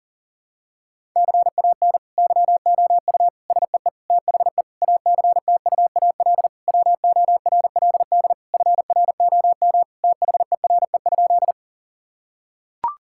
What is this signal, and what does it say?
Morse code, heard as a keyed tone, about 40 wpm, 700 Hz. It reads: CAN YOU SEE THE ACTUAL WORLD FROM THERE?
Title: Sentences from Top 1000 Words - Review Part 5 - 40wpm